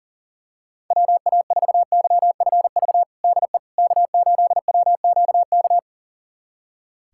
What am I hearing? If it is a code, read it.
Morse code, heard as a keyed tone, about 40 wpm, 700 Hz. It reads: WA4YFV DE K8WQK
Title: USA Callsign DE Callsign 40wpm